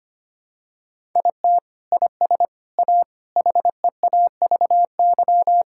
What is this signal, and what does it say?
Morse code, heard as a keyed tone, about 25 wpm, 700 Hz. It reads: IT IS A HEAVY
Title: Sentences from Top 1000 Words - Review Part 6 - 25wpm